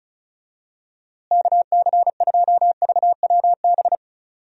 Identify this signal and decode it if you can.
Morse code, heard as a keyed tone, about 34 wpm, 700 Hz. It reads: KC2VWB
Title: USA Call Signs - 6 Characters - Repeated 3x - 35wpm